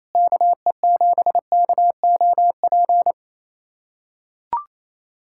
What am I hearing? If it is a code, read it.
Morse code, heard as a keyed tone, about 28 wpm, 700 Hz. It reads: KE7KOP
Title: USA Callsign DE Callsign - 6 Characters - 28wpm